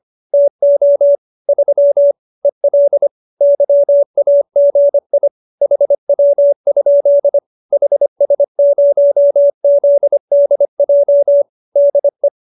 Morse code, heard as a keyed tone, about 25 wpm, 570 Hz. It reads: TO 3 EL YAGI HW? HS0ZDJ DE